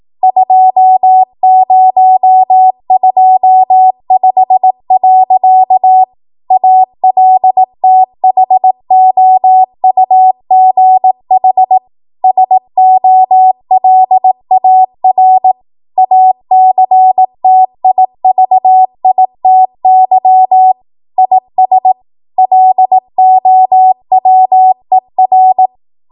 Morse code, heard as a keyed tone, about 18 wpm, 750 Hz. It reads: 2025. ALTHOUGH SOLAR ACTIVITY IS LOWER
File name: Morse / 18WPM